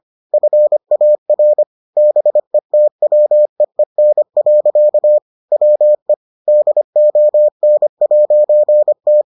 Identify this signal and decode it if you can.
Morse code, heard as a keyed tone, about 25 wpm, 600 Hz. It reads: FAR BETWEEN. WE DON'T